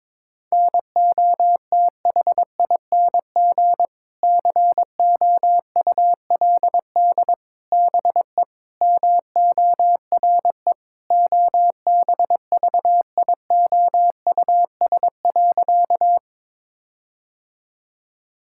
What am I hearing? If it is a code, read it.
Morse code, heard as a keyed tone, about 22 wpm, 700 Hz. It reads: NOTHING COULD BE MORE OBVIOUS.
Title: Sentences from Top 1600 Words 22wpm